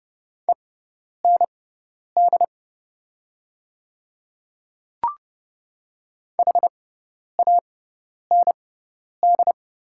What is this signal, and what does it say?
Morse code, heard as a keyed tone, about 29 wpm, spaced as for 10 wpm, 690 Hz. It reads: END HAND
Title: 6x Character Spacing - Top 200 Words - 30wpm